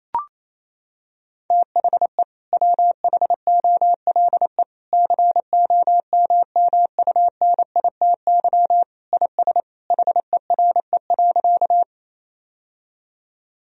Morse code, heard as a keyed tone, about 28 wpm, 700 Hz. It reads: THE WHOLE COMMUNITY IS HERE.
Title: Sentences from Top 1000 Words - Review Part 4 - 28wpm